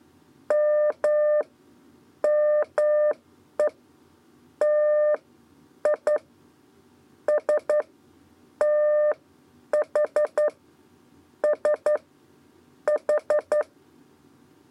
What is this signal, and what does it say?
Morse code, heard as a keyed tone, about 11 wpm, spaced as for 8 wpm, 600 Hz. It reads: M ME TI S TH S H
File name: Lezione trasmessa 2 lez 2 exe